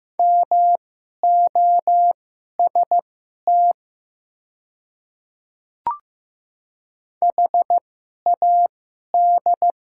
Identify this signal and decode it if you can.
Morse code, heard as a keyed tone, about 15 wpm, spaced as for 11 wpm, 700 Hz. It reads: MOST HAD
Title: 2x Character Spacing - Top 100 Words - 15wpm